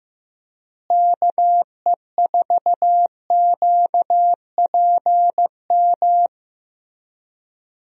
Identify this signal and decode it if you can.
Morse code, heard as a keyed tone, about 15 wpm, 700 Hz. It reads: KE4QPM
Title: USA Call Signs - 6 Characters - Repeated 3x - 15wpm